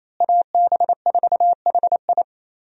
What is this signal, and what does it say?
Morse code, heard as a keyed tone, about 28 wpm, 700 Hz. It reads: AB4HI
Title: USA Call Signs - 5 Characters - Repeated 3x - 28wpm